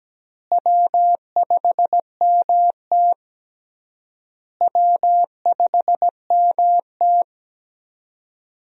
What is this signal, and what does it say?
Morse code, heard as a keyed tone, about 17 wpm, 700 Hz. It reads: W5MT W5MT
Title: USA Call Signs - 4 Characters - Repeated 3x - 17wpm